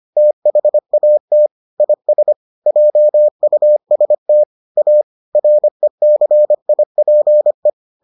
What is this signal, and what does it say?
Morse code, heard as a keyed tone, about 25 wpm, 600 Hz. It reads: THAT IS JUST A RECIPE